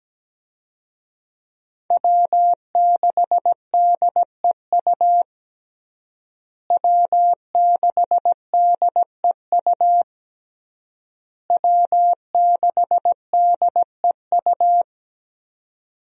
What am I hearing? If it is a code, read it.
Morse code, heard as a keyed tone, about 17 wpm, 700 Hz. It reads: W6DEU W6DEU W6DEU
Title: USA Call Signs - 5 Characters - Repeated 3x - 17wpm